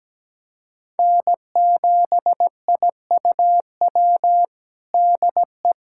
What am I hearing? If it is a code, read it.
Morse code, heard as a keyed tone, about 17 wpm, 700 Hz. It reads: N7IUW DE
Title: USA Callsign DE Callsign - 5 Characters - 17wpm